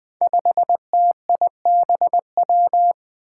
Morse code, heard as a keyed tone, about 20 wpm, 700 Hz. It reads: 5TIBW
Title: Five Challenging Letter-Number Combinations 20wpm